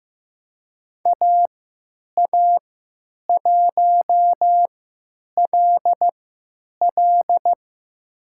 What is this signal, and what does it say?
Morse code, heard as a keyed tone, about 15 wpm, spaced as for 9 wpm, 700 Hz. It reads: AA1LL
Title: USA Call Signs - 5 Characters - 3x Character Spacing - 15wpm